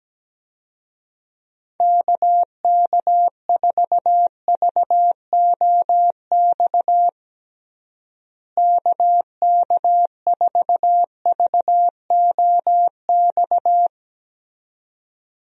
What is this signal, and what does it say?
Morse code, heard as a keyed tone, about 17 wpm, 700 Hz. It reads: KK4VOX KK4VOX
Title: USA Call Signs - 6 Characters - Repeated 2x - 17wpm